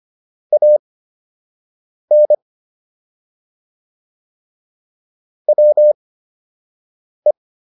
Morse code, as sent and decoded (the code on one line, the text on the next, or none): .- -. / .-- .
AN WE